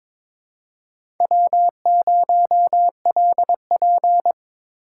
Morse code, heard as a keyed tone, about 22 wpm, 700 Hz. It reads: W0LP
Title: USA Call Signs - 3x Character Spacing - 22wpm